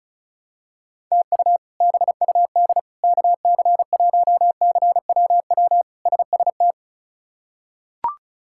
Morse code, heard as a keyed tone, about 36 wpm, 700 Hz. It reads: TU BUD KC1CWW SST
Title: K1USN SST QSO Practice - 35wpm